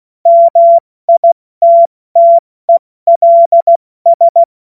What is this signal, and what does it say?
Morse code, heard as a keyed tone, about 16 wpm, 680 Hz. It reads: MITTELS
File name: Morsekurs / Witze / Witz 30. August 2022 70 F80